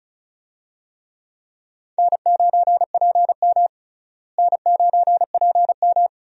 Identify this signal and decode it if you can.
Morse code, heard as a keyed tone, about 34 wpm, 700 Hz. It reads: N9PM N9PM
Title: USA Call Signs - 4 Characters - Repeated 2x - 35wpm